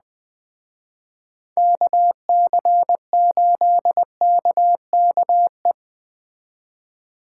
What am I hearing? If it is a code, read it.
Morse code, heard as a keyed tone, about 20 wpm, 700 Hz. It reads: KC8KKE
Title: Speed Racing - USA 2x3 Call Signs 17wpm